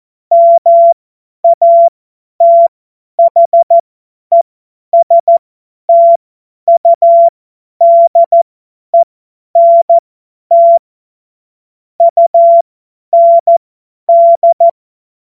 Morse code, heard as a keyed tone, about 14 wpm, spaced as for 10 wpm, 680 Hz. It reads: MATHESTUDENT UND
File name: Morsekurs / Witze / Witz 14. Juni 2022 50 F70